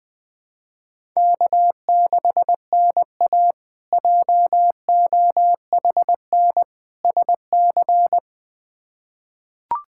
Morse code, heard as a keyed tone, about 20 wpm, 700 Hz. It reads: K6NA JOHN SC
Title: K1USN SST QSO Practice - 20wpm